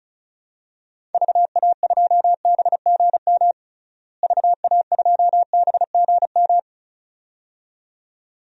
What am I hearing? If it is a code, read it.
Morse code, heard as a keyed tone, about 36 wpm, 700 Hz. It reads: VA2BGM VA2BGM
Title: DXCC-100 Call Signs - 6 Characters - Repeated 2x - 35wpm